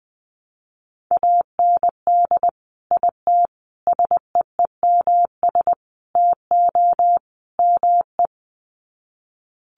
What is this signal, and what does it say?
Morse code, heard as a keyed tone, about 20 wpm, 700 Hz. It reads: AND IT SEEMS TO ME
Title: Sets of 5 Words Encore2 20wpm